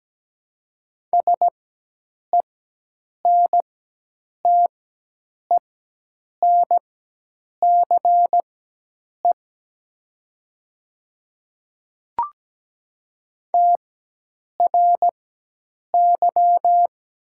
Morse code, heard as a keyed tone, about 17 wpm, spaced as for 8 wpm, 700 Hz. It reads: SENTENCE TRY
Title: 4x Character Spacing - Top 200 Words - 17wpm